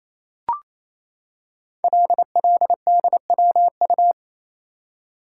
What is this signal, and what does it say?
Morse code, heard as a keyed tone, about 27 wpm, 700 Hz. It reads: LLDWU
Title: Code Groups - 5 Letter:Numbers - 28wpm